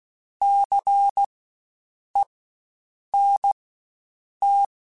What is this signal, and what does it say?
Morse code, heard as a keyed tone, about 16 wpm, spaced as for 7 wpm, 800 Hz. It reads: CENT
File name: Spaced Alphas / Spaced Alphas mp3s / Cs sp @15